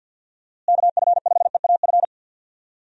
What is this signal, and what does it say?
Morse code, heard as a keyed tone, about 51 wpm, 700 Hz. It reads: KF5EAF
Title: USA Call Signs - 6 Characters - Repeated 2x - 50wpm